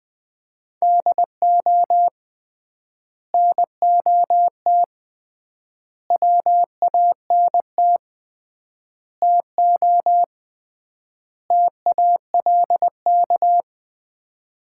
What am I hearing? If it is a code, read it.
Morse code, heard as a keyed tone, about 20 wpm, 700 Hz. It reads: DO NOT WANT TO TALK
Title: 3x Word-Spacing - Sets of 5 Words - 20wpm